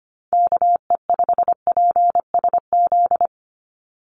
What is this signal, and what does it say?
Morse code, heard as a keyed tone, about 25 wpm, 700 Hz. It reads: KE5PSZ
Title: USA Call Signs 25wpm